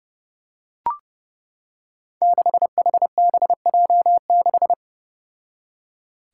Morse code, heard as a keyed tone, about 30 wpm, 700 Hz. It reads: 6HBJ6
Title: Five Challenging Letter-Number Combinations 30wpm